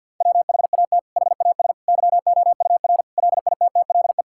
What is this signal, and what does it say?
Morse code, heard as a keyed tone, about 50 wpm, 700 Hz. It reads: WHAT HAS YOUR LITTLE
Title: Sentences from Top 1600 Words 50wpm